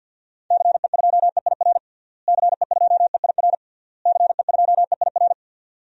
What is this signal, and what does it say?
Morse code, heard as a keyed tone, about 52 wpm, 700 Hz. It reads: CE2EIR CE2EIR CE2EIR
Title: DXCC-100 Call Signs - 6 Characters - Repeated 3x - 50wpm